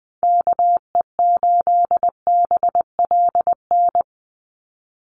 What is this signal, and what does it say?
Morse code, heard as a keyed tone, about 20 wpm, 700 Hz. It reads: KE8BLN
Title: USA Call Signs 20wpm